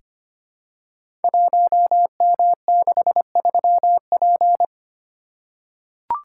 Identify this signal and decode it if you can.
Morse code, heard as a keyed tone, about 25 wpm, 700 Hz. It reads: 1M63P
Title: Code Groups - 5 Letter:Numbers - 25wpm